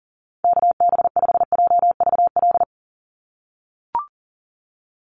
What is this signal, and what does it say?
Morse code, heard as a keyed tone, about 40 wpm, 700 Hz. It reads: KB5JVL
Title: USA Call Signs 40wpm